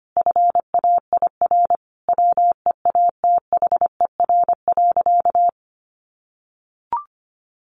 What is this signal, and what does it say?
Morse code, heard as a keyed tone, about 25 wpm, 700 Hz. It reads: FAIR WEATHER.
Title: Sentences from Top 1300 Words 25wpm